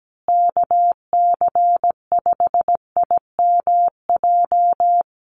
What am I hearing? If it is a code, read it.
Morse code, heard as a keyed tone, about 17 wpm, 700 Hz. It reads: KC5IMJ
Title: USA Call Signs Repeated 3x 17wpm